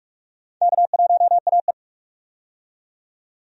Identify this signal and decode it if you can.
Morse code, heard as a keyed tone, about 46 wpm, 700 Hz. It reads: K1AE
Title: USA Call Signs - 4 Characters - 45wpm